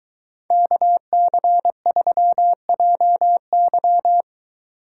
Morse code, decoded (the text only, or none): KC3JY